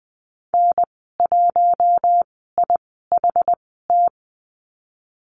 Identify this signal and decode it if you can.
Morse code, heard as a keyed tone, about 20 wpm, spaced as for 14 wpm, 700 Hz. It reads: N1IHT
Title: 082 Lesson 14g - Number 2 - Call Signs 20wpm